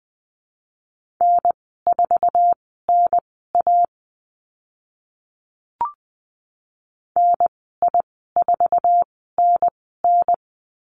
Morse code, heard as a keyed tone, about 20 wpm, spaced as for 14 wpm, 700 Hz. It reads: N4NA NI4NN
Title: 062 Lesson 11g - Letter h - Call Signs 20wpm